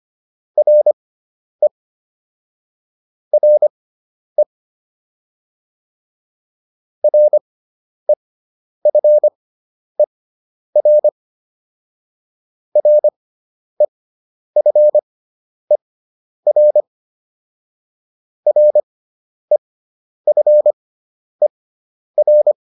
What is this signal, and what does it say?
Morse code, heard as a keyed tone, about 25 wpm, spaced as for 10 wpm, 600 Hz. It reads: RE RE REFER REFER REFER